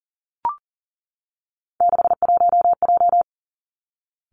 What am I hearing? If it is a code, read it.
Morse code, heard as a keyed tone, about 42 wpm, 700 Hz. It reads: 61J
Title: Three Challenging Letter-Number Combinations 40wpm